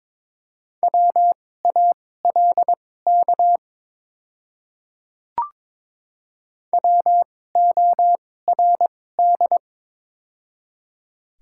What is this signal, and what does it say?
Morse code, heard as a keyed tone, about 22 wpm, spaced as for 16 wpm, 700 Hz. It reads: WALK WORD
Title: 2x Character Spacing - Top 300 Words - 22wpm